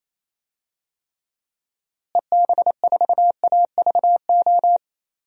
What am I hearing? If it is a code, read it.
Morse code, heard as a keyed tone, about 28 wpm, 700 Hz. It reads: EB4AVO
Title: DXCC-100 Call Signs - 6 Characters - Repeated 3x - 28wpm